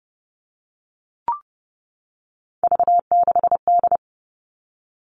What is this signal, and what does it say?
Morse code, heard as a keyed tone, about 30 wpm, 700 Hz. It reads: V6D
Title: All Cut Numbers - 3-Digit 30wpm